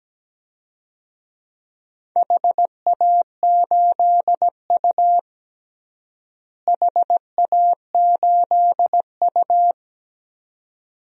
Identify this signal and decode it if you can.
Morse code, heard as a keyed tone, about 17 wpm, 700 Hz. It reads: HA8U HA8U
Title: DXCC-100 Call Signs - 4 Characters - Repeated 2x - 17wpm